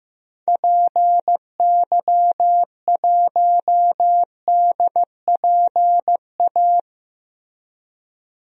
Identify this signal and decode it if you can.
Morse code, heard as a keyed tone, about 15 wpm, 700 Hz. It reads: PY1DPA
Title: DXCC-100 Call Signs - 6 Characters - Repeated 2x - 15wpm